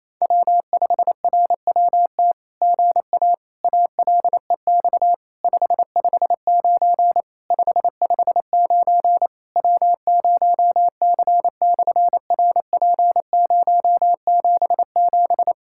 Morse code, heard as a keyed tone, about 27 wpm, 700 Hz. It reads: W5RWT GA ALEX 559 559 W0C/RP077
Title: SOTA QSO Practice 28wpm